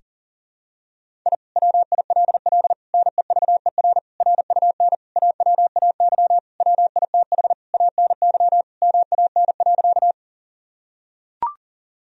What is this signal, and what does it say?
Morse code, heard as a keyed tone, about 40 wpm, 700 Hz. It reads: I WILL NEVER RUN AWAY WITH ANY MAN.
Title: Sentences from Top 1000 Words - Review Part 1 - 40wpm